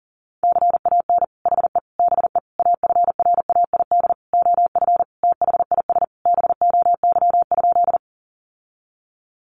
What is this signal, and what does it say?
Morse code, heard as a keyed tone, about 40 wpm, 700 Hz. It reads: CAN HE BE AFRAID OF THIS BOY?
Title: Sentences from Top 1500 Words 40wpm